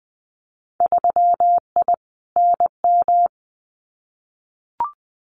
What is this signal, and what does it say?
Morse code, heard as a keyed tone, about 20 wpm, 700 Hz. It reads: 3I NM
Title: Winter Field Day Exchanges Encore 20wpm